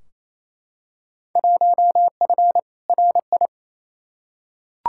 Morse code, heard as a keyed tone, about 28 wpm, 700 Hz. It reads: 1F RI
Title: Field Day Exchanges Encore - 28wpm